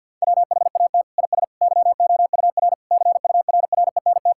Morse code, heard as a keyed tone, about 50 wpm, 700 Hz. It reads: WHAT IS YOUR CURRENT
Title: Sentences from Top 1000 Words - Review Part 6 - 50wpm